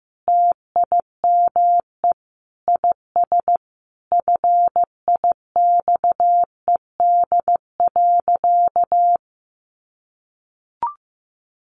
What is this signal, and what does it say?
Morse code, heard as a keyed tone, about 15 wpm, 700 Hz. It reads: TIME IS FIXED.
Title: Sentences from Top 1100 Words 15wpm